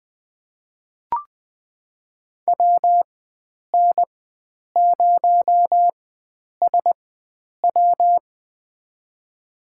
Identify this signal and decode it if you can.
Morse code, heard as a keyed tone, about 20 wpm, spaced as for 9 wpm, 700 Hz. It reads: WN0SW
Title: USA Call Signs - 5 Characters - 4x Character Spacing - 20wpm